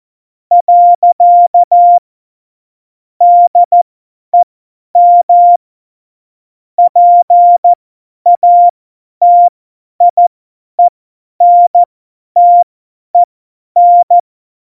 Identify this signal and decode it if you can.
Morse code, heard as a keyed tone, about 14 wpm, spaced as for 10 wpm, 700 Hz. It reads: . DEM PATIENTEN
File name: Morsekurs / Witze / Witz 13 March 2018, 50 F70